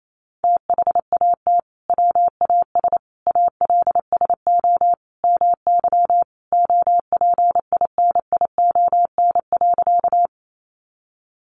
Morse code, heard as a keyed tone, about 28 wpm, 700 Hz. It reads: THAT WAS ALSO MY OPINION.
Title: Sentences from Top 1200 Words 28wpm